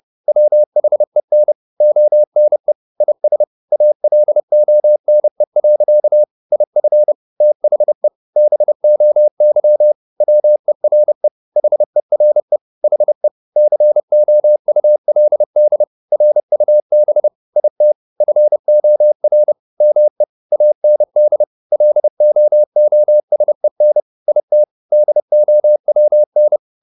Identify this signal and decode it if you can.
Morse code, heard as a keyed tone, about 30 wpm, 600 Hz. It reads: WHEN ONE IS ALONE. IF THE BOY WERE HERE HE COULD RUB IT FOR ME AND LOOSEN IT DOWN
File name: the old man and the sea / the old man and the sea 1 30wpm